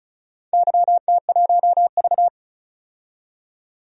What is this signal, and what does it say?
Morse code, heard as a keyed tone, about 36 wpm, 700 Hz. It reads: YT1V
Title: DXCC-100 Call Signs - 4 Characters - 35wpm